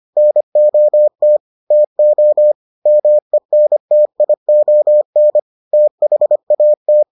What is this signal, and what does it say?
Morse code, heard as a keyed tone, about 25 wpm, 600 Hz. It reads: NOT TO MENTION THAT